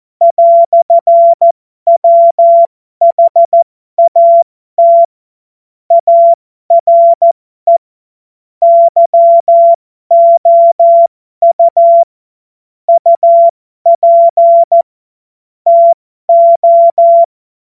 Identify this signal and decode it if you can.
Morse code, heard as a keyed tone, about 14 wpm, spaced as for 12 wpm, 680 Hz. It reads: "WHAT ARE YOU UP TO
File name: Morsekurs / Witze